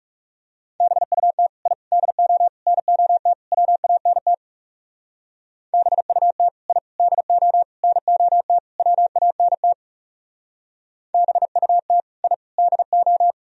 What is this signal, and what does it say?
Morse code, heard as a keyed tone, about 38 wpm, 700 Hz. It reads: BUT I DO NOT WANT BUT I DO NOT WANT BUT I DO